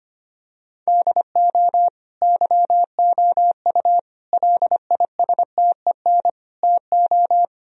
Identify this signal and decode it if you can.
Morse code, heard as a keyed tone, about 25 wpm, 700 Hz. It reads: DO YOU LISTEN TO
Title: Sentences from Top 1000 Words - Review Part 3 - 25wpm